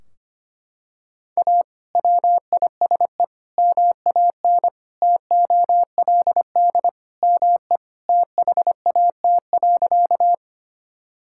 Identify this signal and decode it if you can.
Morse code, heard as a keyed tone, about 25 wpm, 700 Hz. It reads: A WISE MAN TOLD ME THAT.
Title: Sentences from Top 1600 Words 25wpm